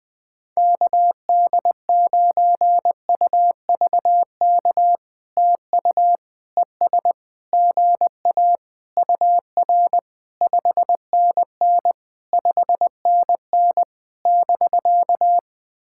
Morse code, heard as a keyed tone, about 20 wpm, 700 Hz. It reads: KD9UVK TU ES GA UR 5NN 5NN <BK>
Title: POTA QSO Practice - 20wpm